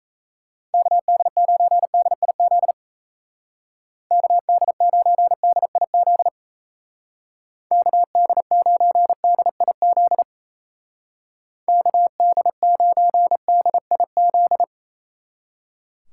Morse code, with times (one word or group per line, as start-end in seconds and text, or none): 0.7-2.7 KD9DIZ
4.1-6.3 KD9DIZ
7.7-10.2 KD9DIZ
11.7-14.6 KD9DIZ